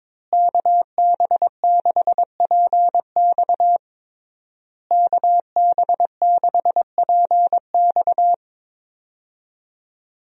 Morse code, heard as a keyed tone, about 21 wpm, 700 Hz. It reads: KB6PX KB6PX
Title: USA Call Signs - 5 Characters - Repeated 3x - 22wpm